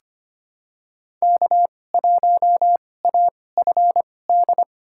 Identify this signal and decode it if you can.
Morse code, heard as a keyed tone, about 25 wpm, spaced as for 18 wpm, 700 Hz. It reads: K1AFD
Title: USA Call Signs - 5 Characters - 2x Character Spacing - 25wpm